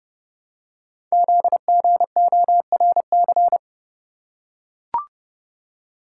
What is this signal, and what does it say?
Morse code, heard as a keyed tone, about 30 wpm, 700 Hz. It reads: ZGORC
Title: Code Groups - 5 Letters - 30wpm